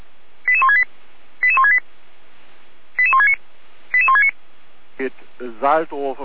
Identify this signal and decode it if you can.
Five-tone selective call (ZVEI), heard as quick runs of five tones, tones of about 70 ms, with speech. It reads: 88168 88168 88169 88169